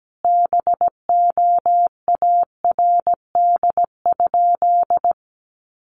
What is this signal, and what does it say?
Morse code, heard as a keyed tone, about 17 wpm, 700 Hz. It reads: BOARD?
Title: Sentences from Top 1300 Words 17wpm